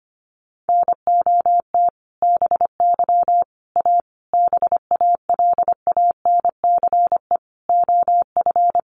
Morse code, heard as a keyed tone, about 25 wpm, 700 Hz. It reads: NOT BY A BALANCE OF
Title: Sentences from Top 900 Words 25wpm